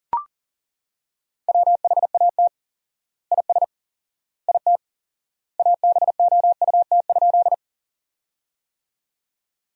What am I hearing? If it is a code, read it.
Morse code, heard as a keyed tone, about 40 wpm, 700 Hz. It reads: WHAT IS IT ABOUT?
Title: 4x Word-Spacing - Sentences from Top 100 Words - 40wpm